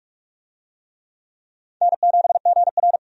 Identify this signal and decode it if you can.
Morse code, heard as a keyed tone, about 47 wpm, 700 Hz. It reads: N7GR
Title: USA Call Signs - 4 Characters - 45wpm